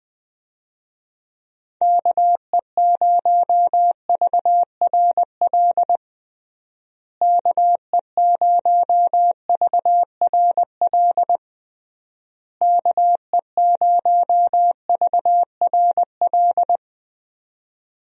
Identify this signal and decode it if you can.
Morse code, heard as a keyed tone, about 20 wpm, 700 Hz. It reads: KE0VRL KE0VRL KE0VRL
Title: USA Call Signs - 6 Characters - Repeated 3x - 20wpm